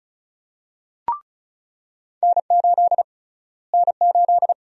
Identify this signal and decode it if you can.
Morse code, heard as a keyed tone, about 36 wpm, 700 Hz. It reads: N8 N8
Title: USA Call Sign Prefixes - Repeated 2x - 35wpm